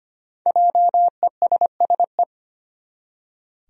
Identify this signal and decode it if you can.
Morse code, heard as a keyed tone, about 25 wpm, 700 Hz. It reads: JESSE
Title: USA Common Names - Max 5 Letters - 25wpm